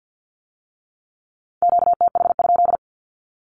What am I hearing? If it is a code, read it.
Morse code, heard as a keyed tone, about 50 wpm, 700 Hz. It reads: QTH?
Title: Q-Codes with Spoken Meaning - 50wpm